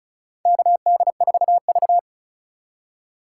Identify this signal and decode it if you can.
Morse code, heard as a keyed tone, about 34 wpm, 700 Hz. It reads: KD4V
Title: USA Call Signs - 4 Characters - 2x Character Spacing - 35wpm